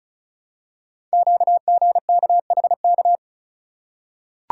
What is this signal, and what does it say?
Morse code, heard as a keyed tone, about 36 wpm, 690 Hz. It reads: QGKHK
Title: Code Groups - 5 Letters - 35wpm